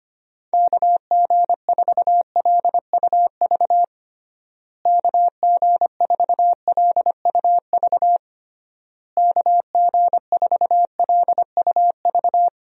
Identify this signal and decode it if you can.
Morse code, heard as a keyed tone, about 25 wpm, 700 Hz. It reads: KG4LUV KG4LUV KG4LUV
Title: USA Call Signs - 6 Characters - Repeated 3x - 25wpm